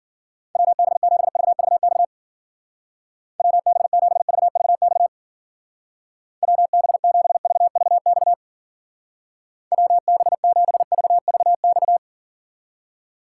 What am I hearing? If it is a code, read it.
Morse code, heard as a keyed tone, about 52 wpm, 700 Hz. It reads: WB7VVX WB7VVX WB7VVX WB7VVX